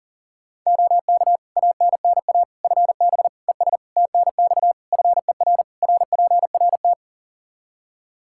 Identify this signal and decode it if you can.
Morse code, heard as a keyed tone, about 40 wpm, 700 Hz. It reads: OK ANNA FB ES TNX FER RPRT
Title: Practice QSO - Exchange 2 - Info - 40wpm